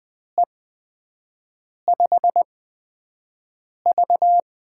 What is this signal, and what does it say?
Morse code, heard as a keyed tone, about 20 wpm, spaced as for 5 wpm, 700 Hz. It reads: E5V
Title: USA Call Signs - 4 Characters - 8x Character Spacing - 20wpm